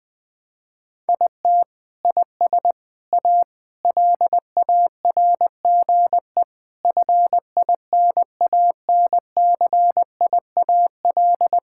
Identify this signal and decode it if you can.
Morse code, heard as a keyed tone, about 20 wpm, 700 Hz. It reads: IT IS A LARGE FINANCIAL
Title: Sentences from Top 1000 Words - Review Part 5 - 20wpm